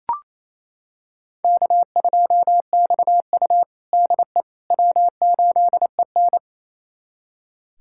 Morse code, heard as a keyed tone, about 28 wpm, 700 Hz. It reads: K2XU DE W8EN
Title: USA Callsign DE Callsign - 4 Characters - 28wpm